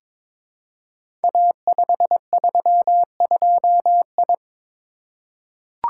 Morse code, heard as a keyed tone, about 23 wpm, 700 Hz. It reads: A532I
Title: Code Groups - 5 Letter:Numbers - 22wpm